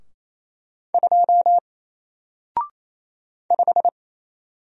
Morse code, heard as a keyed tone, about 28 wpm, 700 Hz. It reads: 2 5
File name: Single-Letter-Number-28wpm